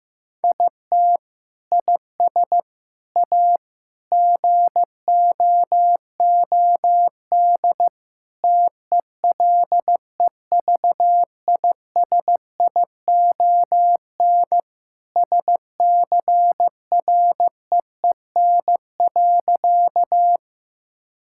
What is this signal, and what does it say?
Morse code, heard as a keyed tone, about 16 wpm, 700 Hz. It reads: IT IS A GOOD TELEVISION SCREEN.
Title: Sentences from Top 1000 Words - Review Part 6 - 15wpm